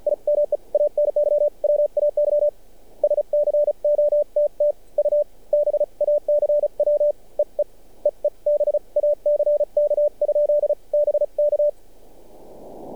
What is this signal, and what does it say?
Morse code, heard as a keyed tone, about 46 wpm, spaced as for 36 wpm, 590 Hz. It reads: I MEAN YWAY SCOTT U BACW EE EE BACK? BK
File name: archived / CRSnet